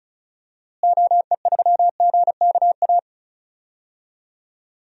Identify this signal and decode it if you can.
Morse code, heard as a keyed tone, about 34 wpm, 700 Hz. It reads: OE3GKA
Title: DXCC-100 Call Signs - 6 Characters - Repeated 2x - 35wpm